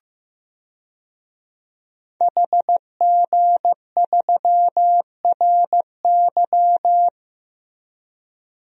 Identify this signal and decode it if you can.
Morse code, heard as a keyed tone, about 15 wpm, 700 Hz. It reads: HG3RY
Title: DXCC-100 Call Signs - 5 Characters - Repeated 3x - 15wpm